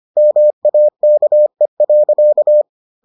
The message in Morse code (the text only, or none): MAKE.